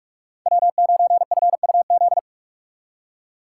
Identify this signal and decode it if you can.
Morse code, heard as a keyed tone, about 46 wpm, 700 Hz. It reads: W9FUZ
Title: USA Call Signs - 5 Characters - 2x Character Spacing - 45wpm